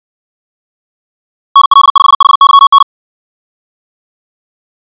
Morse code, heard as a keyed tone, about 56 wpm, 1110 Hz. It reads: NL5HYN
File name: ja7uhv / SampleHST2